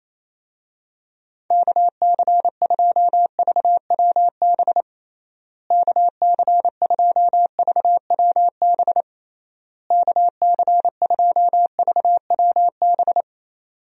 Morse code, heard as a keyed tone, about 28 wpm, 700 Hz. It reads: KC2VWB KC2VWB KC2VWB
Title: USA Call Signs - 6 Characters - Repeated 3x - 28wpm